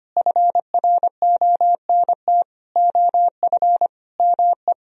Morse code, heard as a keyed tone, about 30 wpm, 700 Hz. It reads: FRONT OF ME
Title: Speed-Racing - Sets of 4 Words 25wpm